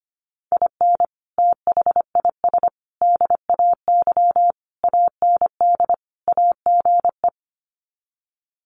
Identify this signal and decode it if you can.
Morse code, heard as a keyed tone, about 25 wpm, 700 Hz. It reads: IN THIS DAY AND AGE
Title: Sets of 5 Words Encore 25wpm